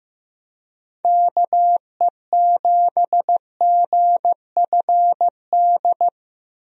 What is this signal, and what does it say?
Morse code, heard as a keyed tone, about 15 wpm, 700 Hz. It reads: KE7GFD
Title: USA Call Signs - 6 Characters - Repeated 3x - 15wpm